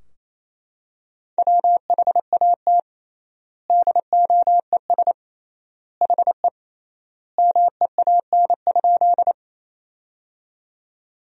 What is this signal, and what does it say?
Morse code, heard as a keyed tone, about 28 wpm, 700 Hz. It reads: WHAT DOES HE MEAN?
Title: 3x Word-Spacing - Sentences from Top 200 Words - 28wpm